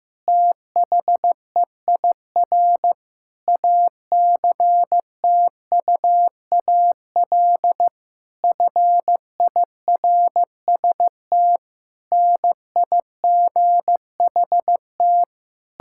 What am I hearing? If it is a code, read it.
Morse code, heard as a keyed tone, about 15 wpm, 700 Hz. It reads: THEIR ACTUAL FIRST NIGHT
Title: Sentences from Top 1000 Words - Review Part 5 - 15wpm